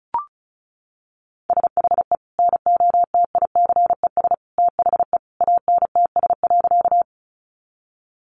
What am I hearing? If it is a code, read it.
Morse code, heard as a keyed tone, about 34 wpm, 700 Hz. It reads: SHE NOTICES THE ANTS.
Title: Easy Phrases 35wpm